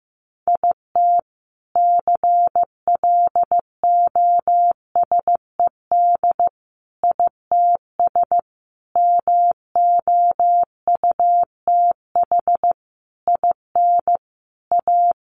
Morse code, heard as a keyed tone, about 15 wpm, 700 Hz. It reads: IT CLOSED ITS MOUTH IN A
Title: A Sound Of Thunder 15wpm